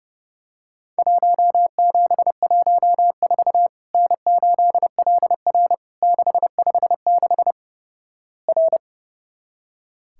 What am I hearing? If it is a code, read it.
Morse code, heard as a keyed tone, about 29 wpm, 700 Hz. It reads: 1714 N8LR 656 R
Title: Worked All Europe DX Contest - QTC Exchange - 30wpm